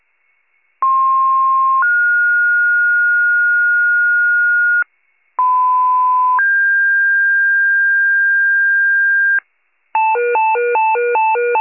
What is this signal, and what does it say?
Two-tone page, heard as a one-second tone, then a three-second tone, about 1060 then 1430 Hz, about 1010 then 1600 Hz, with noise. It is heard from 0.8 to 9.4 s.